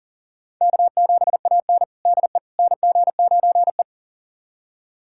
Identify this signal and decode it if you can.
Morse code, heard as a keyed tone, about 40 wpm, 700 Hz. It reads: K7AN DE NG9E
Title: USA Callsign DE Callsign - 4 Characters - 40wpm